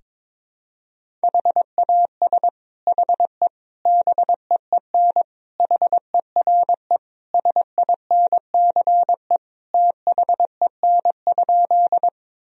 Morse code, heard as a keyed tone, about 22 wpm, 700 Hz. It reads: HAS HE BEEN HERE SINCE THEN?
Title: Sentences from Top 1000 Words - Review Part 1 - 22wpm